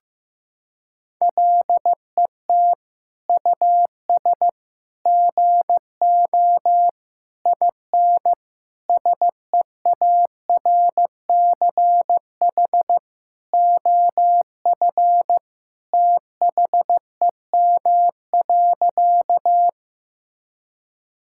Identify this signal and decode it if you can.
Morse code, heard as a keyed tone, about 16 wpm, 700 Hz. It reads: LET US GO IN SEARCH OF THEM.
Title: Sentences from Top 1000 Words - Review Part 5 - 15wpm